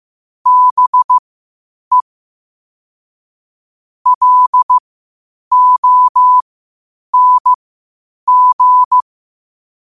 Morse code, heard as a keyed tone, about 15 wpm, spaced as for 8 wpm, 1000 Hz. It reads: BE LONG
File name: qotdfiles / 8wpm